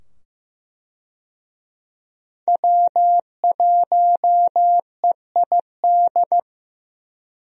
Morse code, heard as a keyed tone, about 15 wpm, 700 Hz. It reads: W1EID
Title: USA Call Signs - 5 Characters - 15wpm